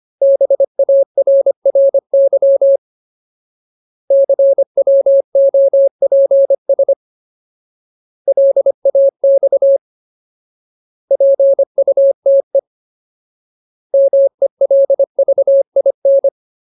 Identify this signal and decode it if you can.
Morse code, heard as a keyed tone, about 25 wpm, 550 Hz. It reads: BARRY CWOPS LAX PUTE MELVIN
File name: Ultimate Word List / 25WPM Ultimate